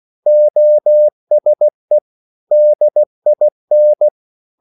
Morse code, heard as a keyed tone, about 16 wpm, 600 Hz. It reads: OSE DIN
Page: Korte danske ord 16wpm